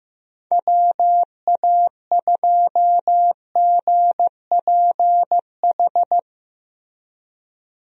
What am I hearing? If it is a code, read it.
Morse code, heard as a keyed tone, about 15 wpm, 700 Hz. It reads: WA2GPH
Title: USA Call Signs - 6 Characters - Repeated 3x - 15wpm